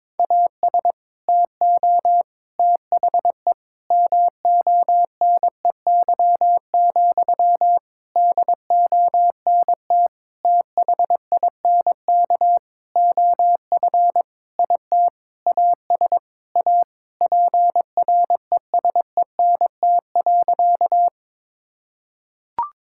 Morse code, heard as a keyed tone, about 22 wpm, 700 Hz. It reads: AS TO THE MONEY, DONT THINK OF IT AS A PRESENT.
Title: Sentences from Top 1000 Words - Review Part 5 - 22wpm